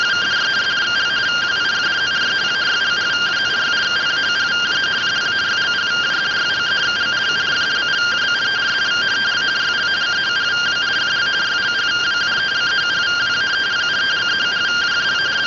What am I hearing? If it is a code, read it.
Two-tone teleprinter signal, about 170 Hz shift.
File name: findings / CT 205 ax mips, iDose 3 0005.gen-22 rtty rx